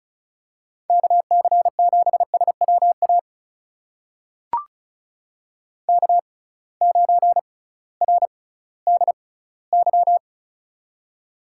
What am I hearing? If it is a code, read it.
Morse code, heard as a keyed tone, about 36 wpm, spaced as for 19 wpm, 700 Hz. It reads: KC7SWA K9RDY
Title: USA Call Signs - 6x Character Spacing - 35wpm